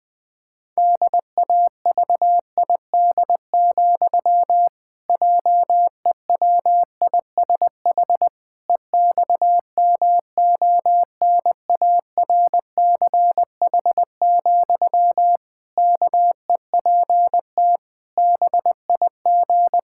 Morse code, heard as a keyed tone, about 20 wpm, 700 Hz. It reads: DAVID, JEWISH EXMONARCH, KEPT BIG
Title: Pangram Challenge v2 - 20wpm